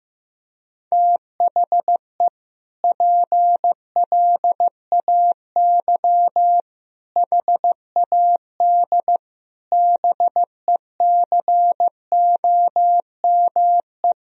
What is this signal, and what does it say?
Morse code, heard as a keyed tone, about 15 wpm, 700 Hz. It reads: THE PLAY HAD BECOME
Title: Sentences from Top 1000 Words - Review Part 4 - 15wpm